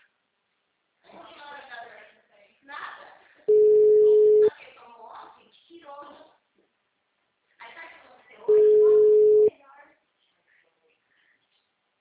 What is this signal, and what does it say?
Ringback tone (Europe, 425 Hz), with speech.